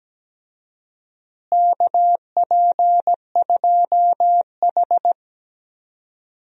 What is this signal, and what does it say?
Morse code, heard as a keyed tone, about 17 wpm, 700 Hz. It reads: KP2H
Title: DXCC-100 Call Signs - 4 Characters - Repeated 2x - 17wpm